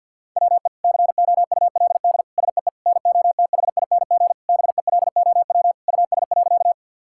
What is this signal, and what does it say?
Morse code, heard as a keyed tone, about 50 wpm, 700 Hz. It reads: WE COULD SEE NOTHING BELOW US.
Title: Sentences from Top 1000 Words - Review Part 5 - 50wpm